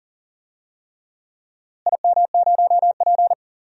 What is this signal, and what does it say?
Morse code, heard as a keyed tone, about 40 wpm, 700 Hz. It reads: IM0P
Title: DXCC-100 Call Signs - 4 Characters - Repeated 3x - 40wpm